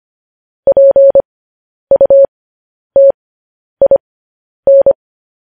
Morse code, heard as a keyed tone, about 25 wpm, spaced as for 10 wpm, 550 Hz. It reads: PUTIN
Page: news headlines in morse code at 10 wpm